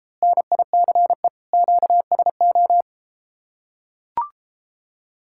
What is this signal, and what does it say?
Morse code, heard as a keyed tone, about 33 wpm, 700 Hz. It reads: NICE QSO